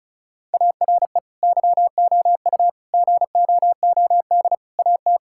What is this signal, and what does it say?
Morse code, heard as a keyed tone, about 36 wpm, 700 Hz. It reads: ARE YOU GOOD AT
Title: Sentences from Top 1000 Words - Review Part 6 - 35wpm